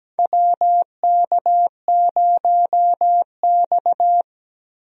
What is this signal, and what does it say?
Morse code, heard as a keyed tone, about 17 wpm, 700 Hz. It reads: WK0X
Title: USA Call Signs - 4 Characters - Repeated 2x - 17wpm